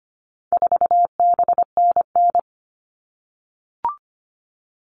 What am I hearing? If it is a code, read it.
Morse code, heard as a keyed tone, about 25 wpm, 700 Hz. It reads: 4BNN
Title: All Cut Numbers - 4-Digit 25wpm